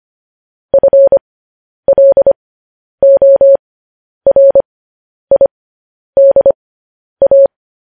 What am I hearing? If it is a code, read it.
Morse code, heard as a keyed tone, about 25 wpm, spaced as for 10 wpm, 550 Hz. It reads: FLORIDA